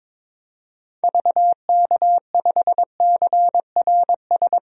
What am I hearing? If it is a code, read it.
Morse code, heard as a keyed tone, about 22 wpm, 700 Hz. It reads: VK5CRS
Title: DXCC-100 Call Signs - 6 Characters - 22wpm